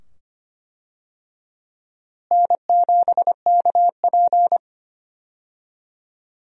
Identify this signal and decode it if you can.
Morse code, heard as a keyed tone, about 25 wpm, 700 Hz. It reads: N7KP
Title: USA Call Signs - 4 Characters - Repeated 2x - 25wpm